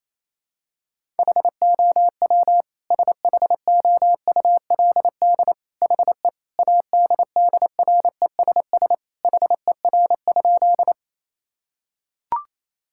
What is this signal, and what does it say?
Morse code, heard as a keyed tone, about 28 wpm, 700 Hz. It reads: HOW SHOULD HE ADDRESS HER?
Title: Sentences from Top 1000 Words - Review Part 6 - 28wpm